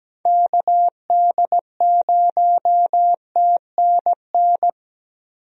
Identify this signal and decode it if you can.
Morse code, heard as a keyed tone, about 17 wpm, 700 Hz. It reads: KD0TNN
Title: USA Call Signs - 6 Characters - Repeated 3x - 17wpm